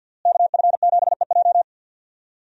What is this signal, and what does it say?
Morse code, heard as a keyed tone, about 52 wpm, 700 Hz. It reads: KF7EJ
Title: USA Call Signs - 5 Characters - Repeated 3x - 50wpm